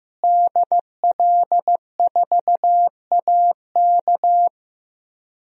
Morse code, heard as a keyed tone, about 15 wpm, 700 Hz. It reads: DL4AK
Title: DXCC-100 Call Signs - 5 Characters - Repeated 2x - 15wpm